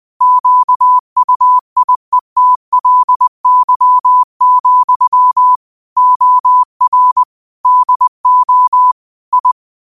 Morse code, heard as a keyed tone, about 20 wpm, 1000 Hz. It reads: QUIETLY, OR DO I